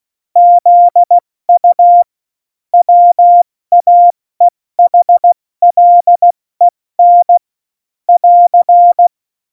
Morse code, heard as a keyed tone, about 16 wpm, 700 Hz. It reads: ZU WAEHLEN <AR>
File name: Morsekurs / Witze / Witz 02. Oktober 2018, 70 F80